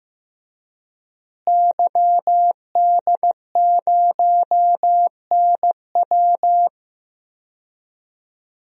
Morse code, heard as a keyed tone, about 15 wpm, 700 Hz. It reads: YD0NW
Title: DXCC-100 Call Signs - 5 Characters - Repeated 3x - 15wpm